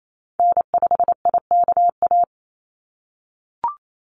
Morse code, heard as a keyed tone, about 28 wpm, 700 Hz. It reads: N5IKA
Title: USA Call Signs 28wpm